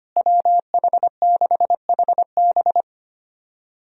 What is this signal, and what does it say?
Morse code, heard as a keyed tone, about 25 wpm, 700 Hz. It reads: WH6HB
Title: USA Call Signs - 5 Characters - Repeated 3x - 25wpm